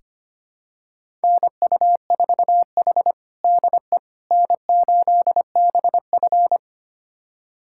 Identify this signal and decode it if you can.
Morse code, heard as a keyed tone, about 25 wpm, 700 Hz. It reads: NU4H DE N8BF
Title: USA Callsign DE Callsign - 4 Characters - 25wpm